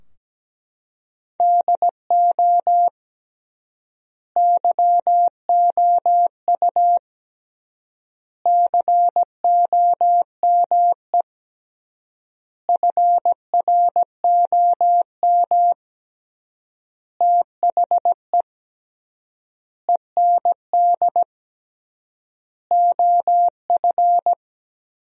Morse code, heard as a keyed tone, about 17 wpm, 700 Hz. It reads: DO YOU COME FROM THE END OF
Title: 3x Word-Spacing - Sentences from Top 200 Words - 17wpm